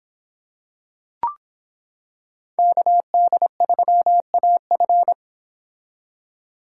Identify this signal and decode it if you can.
Morse code, heard as a keyed tone, about 25 wpm, 700 Hz. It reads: KD3AF